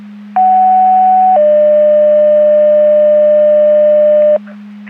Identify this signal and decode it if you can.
Two-tone page, heard as a one-second tone, then a three-second tone, about 750 then 600 Hz, with noise. It is heard from 0.4 to 4.4 s.
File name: Portage 154.385 Paging